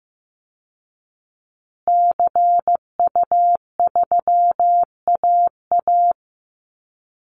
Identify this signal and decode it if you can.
Morse code, heard as a keyed tone, about 15 wpm, 700 Hz. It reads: CU3AA
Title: Call Signs Repeated 3x - 15wpm